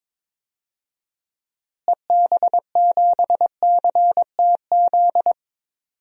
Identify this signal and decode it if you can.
Morse code, heard as a keyed tone, about 22 wpm, 700 Hz. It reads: EB7CTZ
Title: DXCC-100 Call Signs - 6 Characters - Repeated 3x - 22wpm